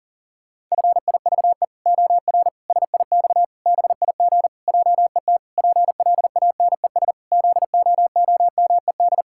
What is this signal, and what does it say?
Morse code, heard as a keyed tone, about 40 wpm, 700 Hz. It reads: FIVE OR SIX BIG JET PLANES ZOOMED
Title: Pangram Challenge v2 - 40wpm